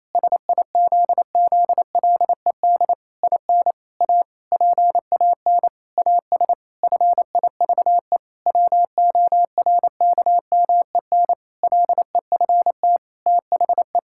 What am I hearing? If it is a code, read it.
Morse code, heard as a keyed tone, about 28 wpm, 700 Hz. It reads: SIZZLED IN A PAN AS FIVE WORKMEN LEFT THE
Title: Pangram Challenge v2 - 28wpm